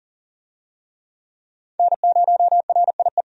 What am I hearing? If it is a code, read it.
Morse code, heard as a keyed tone, about 39 wpm, 700 Hz. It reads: N0RIE
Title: USA Call Signs - 5 Characters - 40wpm